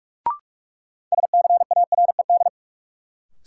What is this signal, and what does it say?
Morse code, heard as a keyed tone, about 46 wpm, 700 Hz. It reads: SCARED
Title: Sentences from Top 700 Words - New Word Drill - 45wpm